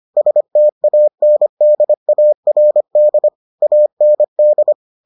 Morse code, heard as a keyed tone, about 25 wpm, 600 Hz. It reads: STANDARD AND